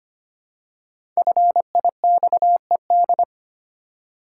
Morse code, heard as a keyed tone, about 25 wpm, 700 Hz. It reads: FIXED
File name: 5 Letter Words - First 3 Letters Given - 25wpm